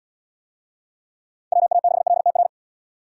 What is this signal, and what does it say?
Morse code, heard as a keyed tone, about 72 wpm, 700 Hz. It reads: WA6LTR